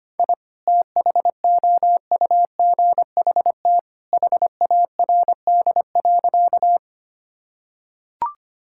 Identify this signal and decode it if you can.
Morse code, heard as a keyed tone, about 25 wpm, 700 Hz. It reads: I THOUGHT HARD.
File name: practice / 25wpm / Sentences with 2-Word Phrases - 25wpm